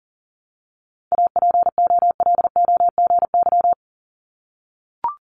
Morse code, heard as a keyed tone, about 38 wpm, 700 Hz. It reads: APOLOGY
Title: Multi-letter Phonemes 40wpm